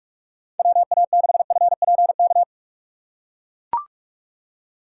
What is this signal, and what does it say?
Morse code, heard as a keyed tone, about 43 wpm, 700 Hz. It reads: WA6FPK
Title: USA Call Signs - 6 Characters - 3x Character Spacing - 45wpm